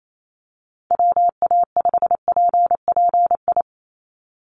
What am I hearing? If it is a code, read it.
Morse code, heard as a keyed tone, about 28 wpm, 700 Hz. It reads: WA5PPI
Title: USA Call Signs 28wpm